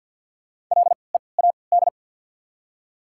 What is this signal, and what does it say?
Morse code, heard as a keyed tone, about 54 wpm, spaced as for 29 wpm, 700 Hz. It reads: LEAD
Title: 3x Character Spacing - Top 500 Words - 50wpm